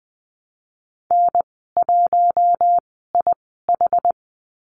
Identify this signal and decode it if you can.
Morse code, heard as a keyed tone, about 20 wpm, spaced as for 15 wpm, 700 Hz. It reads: N1IH
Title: 062 Lesson 11g - Letter h - Call Signs 20wpm